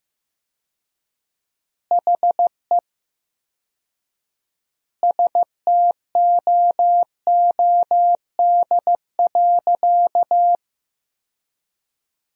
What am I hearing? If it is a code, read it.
Morse code, heard as a keyed tone, about 15 wpm, 700 Hz. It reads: HE STOOD.
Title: 4x Word-Spacing - Sentences from Top 500 Words - 15wpm